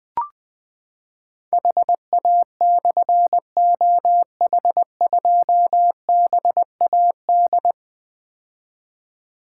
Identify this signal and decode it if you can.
Morse code, heard as a keyed tone, about 21 wpm, 700 Hz. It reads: HA/OH2BAD
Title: Difficult Call Signs 20wpm